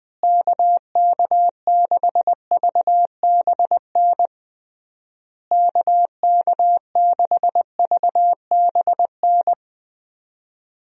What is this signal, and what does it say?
Morse code, heard as a keyed tone, about 21 wpm, 700 Hz. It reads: KK6VBN KK6VBN
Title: USA Call Signs - 6 Characters - Repeated 3x - 20wpm